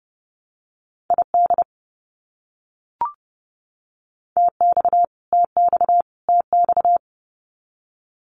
Morse code, heard as a keyed tone, about 30 wpm, 700 Hz. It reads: ID TX TX TX
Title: US States Repeated 3x 30wpm